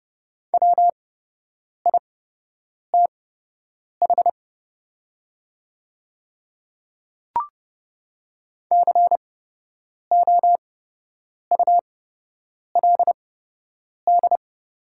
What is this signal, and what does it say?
Morse code, heard as a keyed tone, about 30 wpm, spaced as for 8 wpm, 700 Hz. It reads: WITH COULD
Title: 8x Character Spacing - Top 100 Words - 30wpm